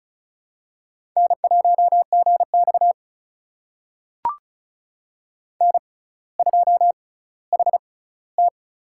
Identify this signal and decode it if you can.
Morse code, heard as a keyed tone, about 36 wpm, spaced as for 18 wpm, 700 Hz. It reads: N1GX N2HT
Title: USA Call Signs - 4 Characters - 6x Character Spacing - 35wpm